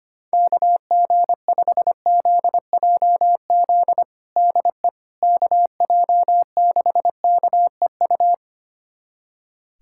Morse code, heard as a keyed tone, about 25 wpm, 700 Hz. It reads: KG5ZJZ DE KJ6KEU
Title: USA Callsign DE Callsign - 6 Characters - 25wpm